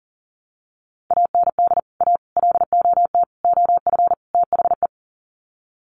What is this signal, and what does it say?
Morse code, heard as a keyed tone, about 40 wpm, 700 Hz. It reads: AND A LOT OF THE
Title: Sets of 5 Words Encore 40wpm